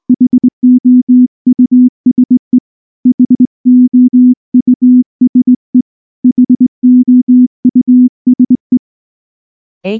Morse code, heard as a keyed tone, about 18 wpm, 250 Hz. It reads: HOUSE HOUSE HOUSE
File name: morse / morse-training-20WPM-3